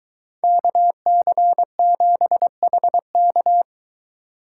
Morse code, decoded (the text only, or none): KC7HK